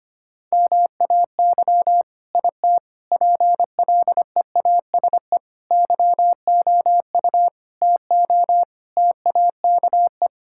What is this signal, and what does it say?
Morse code, heard as a keyed tone, about 25 wpm, 700 Hz. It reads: MAY IT PLEASE YOU TO TAKE
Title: Sentences from Top 1600 Words 25wpm